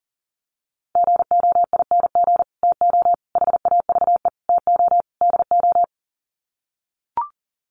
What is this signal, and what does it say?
Morse code, heard as a keyed tone, about 40 wpm, 700 Hz. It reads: GOING TO HAVE TO DO
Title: Sets of 5 Words Encore 40wpm